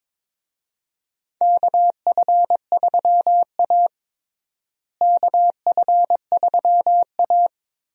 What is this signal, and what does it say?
Morse code, heard as a keyed tone, about 22 wpm, 700 Hz. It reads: KF3A KF3A
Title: USA Call Signs - 4 Characters - Repeated 2x - 22wpm